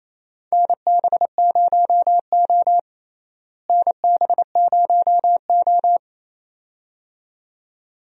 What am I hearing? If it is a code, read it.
Morse code, heard as a keyed tone, about 28 wpm, 700 Hz. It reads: NB0O NB0O
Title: USA Call Signs - 4 Characters - Repeated 2x - 28wpm